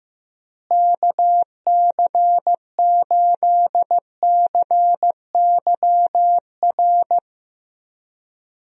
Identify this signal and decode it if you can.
Morse code, heard as a keyed tone, about 15 wpm, 700 Hz. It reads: KC8CYR
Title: USA Call Signs - 6 Characters - Repeated 3x - 15wpm